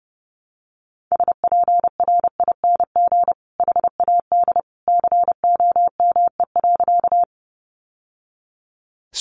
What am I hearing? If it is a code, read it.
Morse code, heard as a keyed tone, about 30 wpm, 700 Hz. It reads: SPRING HAD COME.
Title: Sentences from Top 1200 Words 30wpm